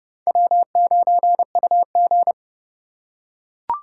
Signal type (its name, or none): Morse code